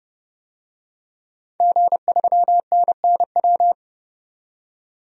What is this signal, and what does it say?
Morse code, heard as a keyed tone, about 30 wpm, 700 Hz. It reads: G3NNW
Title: DXCC-100 Call Signs - 5 Characters - 30wpm